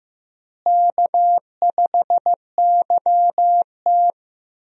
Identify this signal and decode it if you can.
Morse code, heard as a keyed tone, about 15 wpm, 700 Hz. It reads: K5YT
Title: USA Call Signs - 4 Characters - Repeated 2x - 15wpm